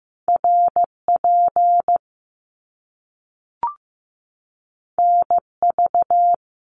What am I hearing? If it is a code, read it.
Morse code, heard as a keyed tone, about 15 wpm, 700 Hz. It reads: RP NV
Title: Top Two Letter Combinations in Words 15wpm